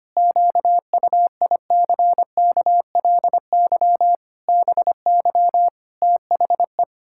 Morse code, heard as a keyed tone, about 25 wpm, 700 Hz. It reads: QUICKLY BY THE
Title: Pangram Challenge v2 - 25wpm